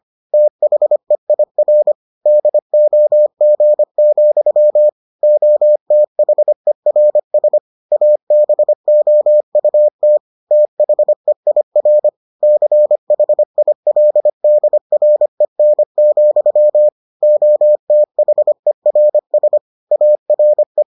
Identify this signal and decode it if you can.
Morse code, heard as a keyed tone, about 25 wpm, 600 Hz. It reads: THEIR DOG, OTHERS ABOUT THEIR CHILDREN, OTHERS ARE